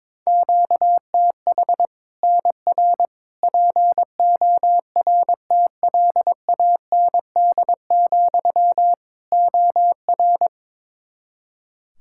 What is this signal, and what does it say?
Morse code, heard as a keyed tone, about 22 wpm, 700 Hz. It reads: QTH NR PORTLAND, OR
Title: Practice QSO - Exchange 1 - Beginning - 22wpm